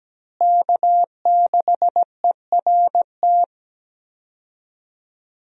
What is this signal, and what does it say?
Morse code, heard as a keyed tone, about 17 wpm, 700 Hz. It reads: K6ERT
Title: USA Call Signs - 5 Characters - Repeated 3x - 17wpm